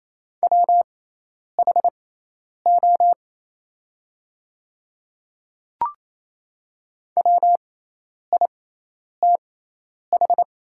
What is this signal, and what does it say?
Morse code, heard as a keyed tone, about 28 wpm, spaced as for 10 wpm, 700 Hz. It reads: WHO WITH
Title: 6x Character Spacing - Top 100 Words - 28wpm